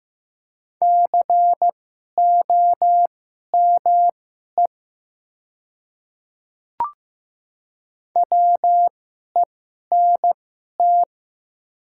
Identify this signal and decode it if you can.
Morse code, heard as a keyed tone, about 15 wpm, spaced as for 11 wpm, 700 Hz. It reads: COME WENT
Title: 2x Character Spacing - Top 500 Words - 15wpm